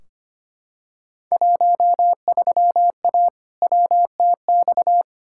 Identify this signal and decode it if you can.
Morse code, heard as a keyed tone, about 25 wpm, 700 Hz. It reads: 13A WTX
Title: Field Day Exchanges Encore - 25wpm